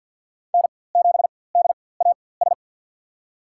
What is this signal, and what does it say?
Morse code, heard as a keyed tone, about 53 wpm, spaced as for 24 wpm, 700 Hz. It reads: N7DAS
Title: USA Call Signs - 4x Character Spacing - 50wpm